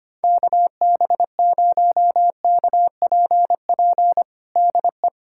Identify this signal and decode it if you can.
Morse code, heard as a keyed tone, about 25 wpm, 700 Hz. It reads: KB0KPP DE
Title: USA Callsign DE Callsign 25wpm